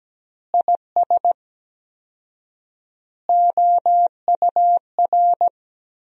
Morse code, heard as a keyed tone, about 17 wpm, 700 Hz. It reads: IS OUR
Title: 4x Word-Spacing - Sentences from Top 500 Words - 17wpm